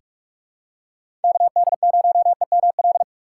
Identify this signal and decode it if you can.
Morse code, heard as a keyed tone, about 46 wpm, 700 Hz. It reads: KD0EML
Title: USA Call Signs - 6 Characters - 3x Character Spacing - 45wpm